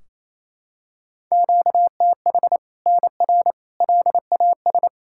Morse code, heard as a keyed tone, about 28 wpm, 700 Hz. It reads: QTH NR LAS
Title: Practice QSO - Exchange 1 - Beginning - 28wpm